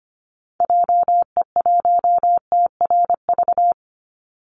Morse code, heard as a keyed tone, about 25 wpm, 700 Hz. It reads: JE1TRV
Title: Call Signs Repeated 2x - 25wpm